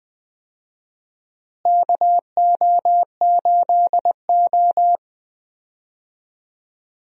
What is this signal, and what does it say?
Morse code, heard as a keyed tone, about 20 wpm, 700 Hz. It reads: KO8O
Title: USA Call Signs - 4 Characters - Repeated 2x - 20wpm